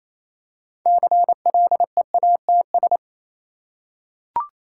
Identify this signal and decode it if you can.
Morse code, heard as a keyed tone, about 27 wpm, 690 Hz. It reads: CLEATS
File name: Easy Words - Speed-Racing with Hints - 22wpm